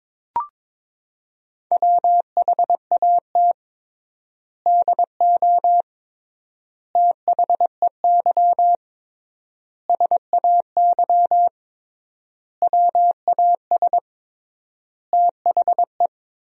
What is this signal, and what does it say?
Morse code, heard as a keyed tone, about 22 wpm, 700 Hz. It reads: WHAT DO THEY SAY WAS THE
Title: 3x Word-Spacing - Sentences from Top 200 Words - 22wpm